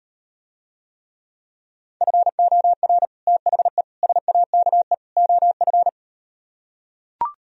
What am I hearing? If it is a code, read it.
Morse code, heard as a keyed tone, about 38 wpm, 700 Hz. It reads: FOR THE SAKE OF